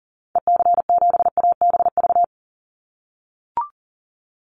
Morse code, heard as a keyed tone, about 40 wpm, 700 Hz. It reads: EC7ABV
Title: Call Signs - 40wpm